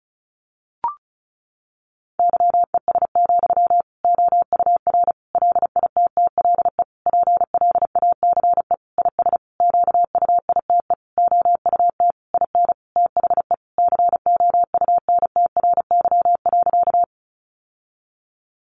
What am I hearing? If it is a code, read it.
Morse code, heard as a keyed tone, about 34 wpm, 700 Hz. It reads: YES, OUR LITTLE PLACE IS QUITE OUT IN THE COUNTRY.
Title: Sentences from Top 700 Words 35wpm